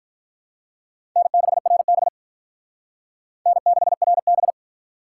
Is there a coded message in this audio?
Yes: Morse code from 1.2 to 4.5 s, heard as a keyed tone, about 45 wpm, 700 Hz. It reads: N6RB N6RB